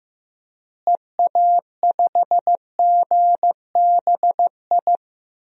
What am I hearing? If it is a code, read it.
Morse code, heard as a keyed tone, about 15 wpm, 700 Hz. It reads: EA5GBI
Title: DXCC-100 Call Signs - 6 Characters - Repeated 3x - 15wpm